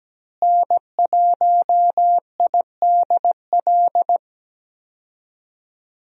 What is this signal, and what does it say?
Morse code, heard as a keyed tone, about 17 wpm, 700 Hz. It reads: N1IDL
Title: USA Call Signs - 5 Characters - Repeated 2x - 17wpm